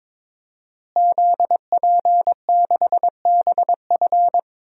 Morse code, heard as a keyed tone, about 22 wpm, 700 Hz. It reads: ZP6BF
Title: DXCC-100 Call Signs - 5 Characters - Repeated 3x - 22wpm